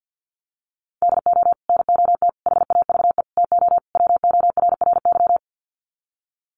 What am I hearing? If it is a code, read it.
Morse code, heard as a keyed tone, about 50 wpm, 700 Hz. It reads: DO NOT HAVE TO WORRY
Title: Sets of 5 Words 50wpm